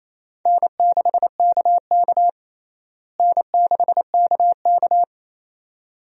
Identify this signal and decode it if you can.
Morse code, heard as a keyed tone, about 28 wpm, 700 Hz. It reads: N6KK N6KK
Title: USA Call Signs - 4 Characters - Repeated 2x - 28wpm